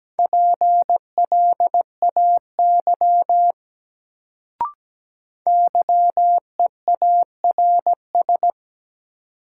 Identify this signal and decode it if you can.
Morse code, heard as a keyed tone, about 17 wpm, 700 Hz. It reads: PLAY YEARS
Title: Sentences from Top 200 Words - New Word Drill - 17wpm